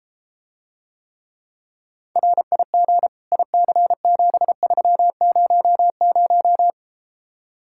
Morse code, heard as a keyed tone, about 33 wpm, 700 Hz. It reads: RIG IC7300